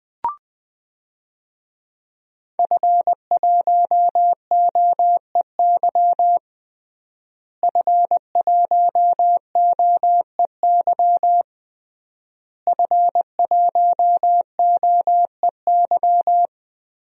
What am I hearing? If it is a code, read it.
Morse code, heard as a keyed tone, about 21 wpm, 700 Hz. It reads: F1OEY F1OEY F1OEY
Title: DXCC-100 Call Signs - 5 Characters - Repeated 3x - 20wpm